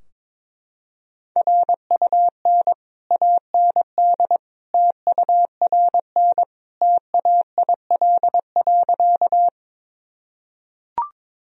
Morse code, heard as a keyed tone, about 22 wpm, 700 Hz. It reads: RUN AND TURN TAIL.
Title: Sentences from Top 1000 Words - Review Part 3 - 22wpm